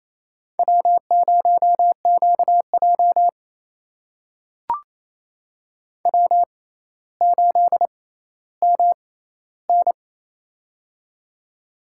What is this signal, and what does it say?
Morse code, heard as a keyed tone, about 29 wpm, spaced as for 15 wpm, 700 Hz. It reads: W0QJ W8MN
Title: USA Call Signs - 4 Characters - 6x Character Spacing - 28wpm